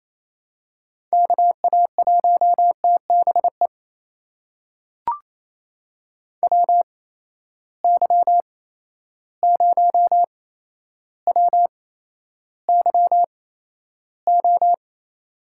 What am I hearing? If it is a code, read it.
Morse code, heard as a keyed tone, about 28 wpm, spaced as for 12 wpm, 700 Hz. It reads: KA1TBE WY0WYO
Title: USA Call Signs - 6 Characters - 8x Character Spacing - 28wpm